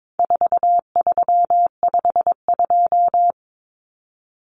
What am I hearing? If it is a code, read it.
Morse code, heard as a keyed tone, about 22 wpm, 700 Hz. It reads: 4352
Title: Common Cut Numbers - 4-Digit 22wpm